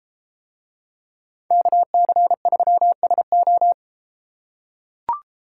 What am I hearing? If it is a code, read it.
Morse code, heard as a keyed tone, about 31 wpm, 700 Hz. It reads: KC3SO